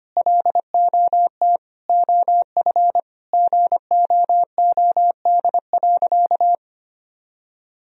Morse code, heard as a keyed tone, about 25 wpm, 700 Hz. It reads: LOT OF GOOD.
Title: Sentences from Top 1000 Words - Review Part 3 - 25wpm